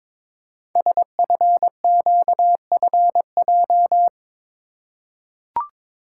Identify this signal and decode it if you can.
Morse code, heard as a keyed tone, about 22 wpm, 700 Hz. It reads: SFQFJ
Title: Code Groups - 5 Letters - 22wpm